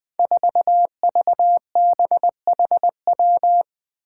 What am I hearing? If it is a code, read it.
Morse code, heard as a keyed tone, about 20 wpm, 700 Hz. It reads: 4VBHW
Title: Five Challenging Letter-Number Combinations 20wpm